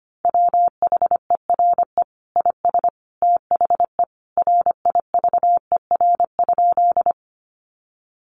Morse code, heard as a keyed tone, about 25 wpm, 700 Hz. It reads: WHERE IS THE RIVER?
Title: Sentences from Top 300 Words 25wpm